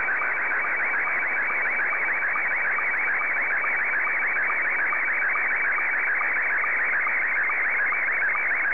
SSTV (slow-scan television), Scottie 1.